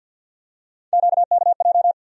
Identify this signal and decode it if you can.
Morse code, heard as a keyed tone, about 48 wpm, 700 Hz. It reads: QKJ
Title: USA Call Sign Postfixes - 3 Letters 50wpm